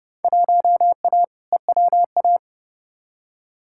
Morse code, heard as a keyed tone, about 30 wpm, 700 Hz. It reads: 1A EWA
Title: Field Day Exchanges Encore - 30wpm